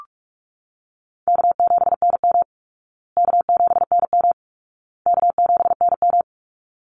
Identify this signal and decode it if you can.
Morse code, heard as a keyed tone, about 44 wpm, 700 Hz. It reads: K7NM K7NM K7NM
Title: Easy Call Signs Repeated 3x - 45wpm